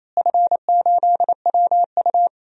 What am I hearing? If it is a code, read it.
Morse code, heard as a keyed tone, about 28 wpm, 700 Hz. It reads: F8WU
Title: DXCC-100 Call Signs - 4 Characters - Repeated 3x - 28wpm